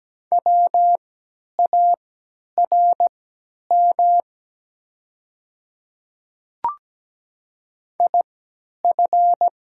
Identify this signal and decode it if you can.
Morse code, heard as a keyed tone, about 17 wpm, spaced as for 10 wpm, 700 Hz. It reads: WARM IF
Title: 3x Character Spacing - Top 500 Words - 17wpm